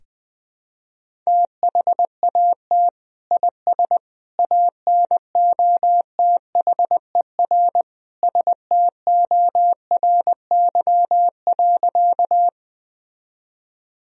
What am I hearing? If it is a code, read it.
Morse code, heard as a keyed tone, about 20 wpm, 700 Hz. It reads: THAT IS ANOTHER STORY.
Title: Sentences from Top 1000 Words - Review Part 1 - 20wpm